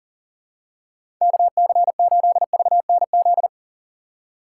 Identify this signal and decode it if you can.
Morse code, heard as a keyed tone, about 40 wpm, 700 Hz. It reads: KC8VNZ
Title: USA Call Signs - 6 Characters - 2x Character Spacing - 40wpm